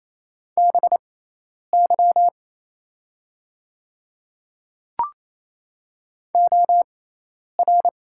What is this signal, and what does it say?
Morse code, heard as a keyed tone, about 28 wpm, spaced as for 10 wpm, 690 Hz. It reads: BY OR
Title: 6x Character Spacing - Top 100 Words - 28wpm